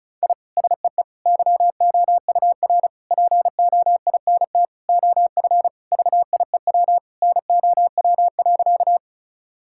Morse code, heard as a keyed tone, about 34 wpm, 700 Hz. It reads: I SEE YOUR POINT OF VIEW NOW.
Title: Sentences from Top 1000 Words - Review Part 6 - 35wpm